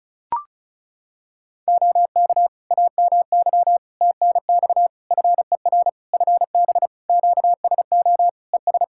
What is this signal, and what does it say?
Morse code, heard as a keyed tone, about 34 wpm, 700 Hz. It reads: OK AMY TNX FER FB QSO ES
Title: Practice QSO - Exchange 4 - Ending - 35wpm